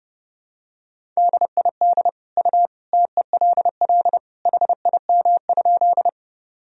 Morse code, heard as a keyed tone, about 32 wpm, 700 Hz. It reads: DID U TELL HIM?
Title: Sentences from Top 200 Words and Abbreviations 30wpm